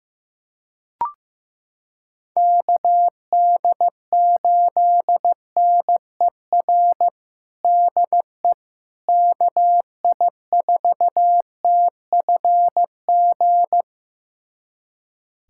Morse code, heard as a keyed tone, about 15 wpm, 700 Hz. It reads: KD8NER DE KI4TFG
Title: USA Callsign DE Callsign - 6 Characters - 15wpm